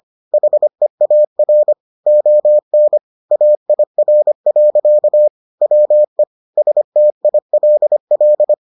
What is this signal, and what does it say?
Morse code, heard as a keyed tone, about 25 wpm, 600 Hz. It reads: HEAR ON AIR. WE STILL